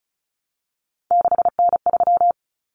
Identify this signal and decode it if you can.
Morse code, heard as a keyed tone, about 34 wpm, 700 Hz. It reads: 6N3
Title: Common Cut Numbers - 3-Digit 35wpm